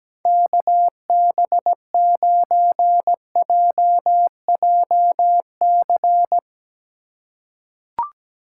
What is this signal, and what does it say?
Morse code, heard as a keyed tone, about 17 wpm, 700 Hz. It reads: KB9JJC
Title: USA Call Signs - 6 Characters - 4x Character Spacing - 17wpm